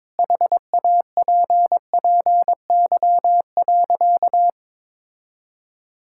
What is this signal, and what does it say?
Morse code, heard as a keyed tone, about 21 wpm, 700 Hz. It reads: HAPPY.
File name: practice / 22wpm / Sentences with 3-Word Phrases - 22wpm